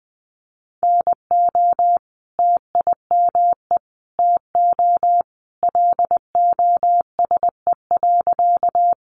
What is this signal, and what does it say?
Morse code, heard as a keyed tone, about 20 wpm, 700 Hz. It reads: NO TIME TO LOSE.
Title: Sentences from Top 1100 Words 20wpm